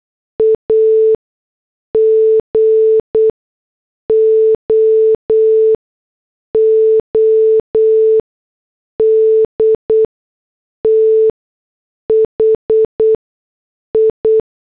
Morse code, heard as a keyed tone, about 8 wpm, spaced as for 6 wpm, 440 Hz. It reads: A GOOD THI